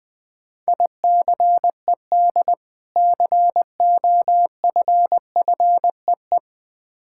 Morse code, heard as a keyed tone, about 20 wpm, 690 Hz. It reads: ICED COFFEE
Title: Difficult Words with Hints - 20wpm